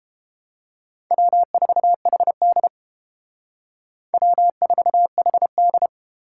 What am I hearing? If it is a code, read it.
Morse code, heard as a keyed tone, about 30 wpm, 700 Hz. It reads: W4HD W4HD